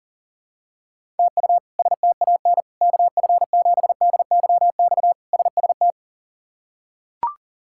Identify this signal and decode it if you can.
Morse code, heard as a keyed tone, about 40 wpm, 700 Hz. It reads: TU STAN KF7DYX SST
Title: K1USN SST QSO Practice - 40wpm